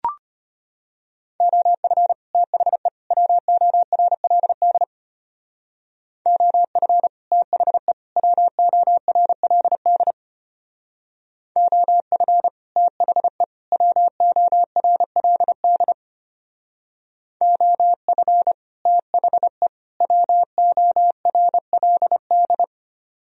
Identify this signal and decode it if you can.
Morse code, heard as a keyed tone, about 33 wpm, 700 Hz. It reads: OF THE WORLD OF THE WORLD OF THE WORLD OF THE WORLD